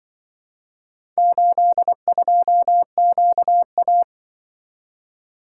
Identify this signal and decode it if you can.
Morse code, heard as a keyed tone, about 24 wpm, 700 Hz. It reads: 82QA
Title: Speed-Racing - Four Letter-Number Combinations - 20wpm